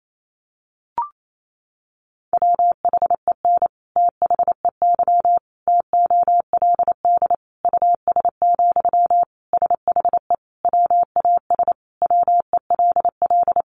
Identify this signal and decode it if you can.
Morse code, heard as a keyed tone, about 28 wpm, 700 Hz. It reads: WHEN THEY TOLD US, SHE WAS WELL
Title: Sentences from Top 400 Words 28wpm